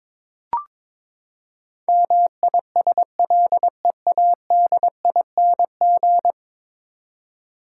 Morse code, heard as a keyed tone, about 22 wpm, 700 Hz. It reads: MISLEADING
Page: morse-code-ninja-practice-22wpm